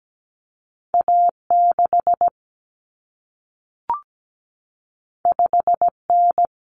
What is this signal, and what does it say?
Morse code, heard as a keyed tone, about 17 wpm, 700 Hz. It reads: A6 5N
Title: Common Cut Numbers - 2-Digit 17wpm